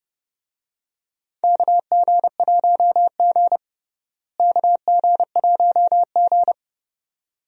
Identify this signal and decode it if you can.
Morse code, heard as a keyed tone, about 30 wpm, 700 Hz. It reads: KG1G KG1G
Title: USA Call Signs - 4 Characters - Repeated 2x - 30wpm